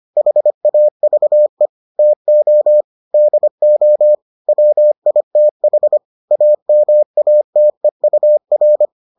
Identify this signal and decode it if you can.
Morse code, heard as a keyed tone, about 25 wpm, 600 Hz. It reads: HAVE TO DO WITH AMATEUR